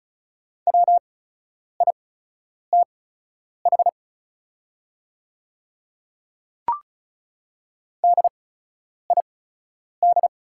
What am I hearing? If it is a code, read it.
Morse code, heard as a keyed tone, about 36 wpm, spaced as for 10 wpm, 700 Hz. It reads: WITH DID
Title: 8x Character Spacing - Top 100 Words - 35wpm